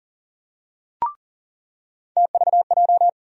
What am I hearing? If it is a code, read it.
Morse code, heard as a keyed tone, about 40 wpm, 700 Hz. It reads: TVJ
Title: USA Call Sign Postfixes - 3 Letters 40wpm